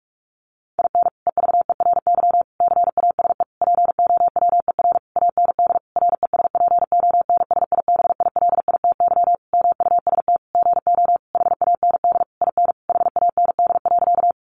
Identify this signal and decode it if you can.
Morse code, heard as a keyed tone, about 44 wpm, 700 Hz. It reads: IN EVERY CASE POWER AND RESPONSIBILITY MUST GO HAND IN HAND.
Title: Sentences from Top 1200 Words 45wpm